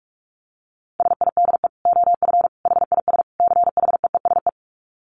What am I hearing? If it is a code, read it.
Morse code, heard as a keyed tone, about 44 wpm, 700 Hz. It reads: SIDE OF HIS CHEESE
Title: Dit Challenge 45wpm